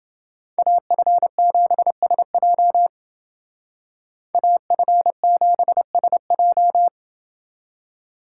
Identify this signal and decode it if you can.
Morse code, heard as a keyed tone, about 29 wpm, 700 Hz. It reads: AF7SJ AF7SJ